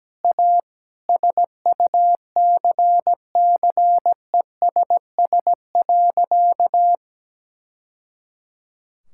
Morse code, heard as a keyed tone, about 17 wpm, 700 Hz. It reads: A SUCCESS.
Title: Sentences from Top 1000 Words - Review Part 5 - 17wpm